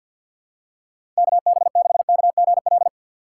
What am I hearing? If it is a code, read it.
Morse code, heard as a keyed tone, about 53 wpm, 700 Hz. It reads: KB6KGL
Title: USA Call Signs - 6 Characters - 3x Character Spacing - 50wpm